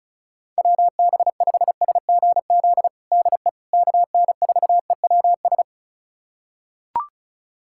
Morse code, heard as a keyed tone, about 36 wpm, 690 Hz. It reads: WB5SGZ DE KN4EWS
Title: USA Callsign DE Callsign 35wpm